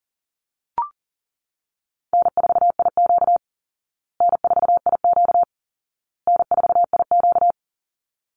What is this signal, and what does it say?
Morse code, heard as a keyed tone, about 40 wpm, 700 Hz. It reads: N4IQ N4IQ N4IQ
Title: Easy Call Signs Repeated 3x - 40wpm